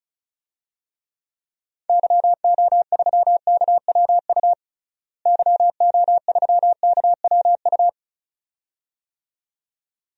Morse code, heard as a keyed tone, about 36 wpm, 700 Hz. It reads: YO3KWU YO3KWU
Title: DXCC-100 Call Signs - 6 Characters - Repeated 2x - 35wpm